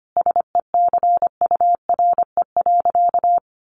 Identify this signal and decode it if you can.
Morse code, heard as a keyed tone, about 25 wpm, 700 Hz. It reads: SECURE.
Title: Sentences from Top 1200 Words 25wpm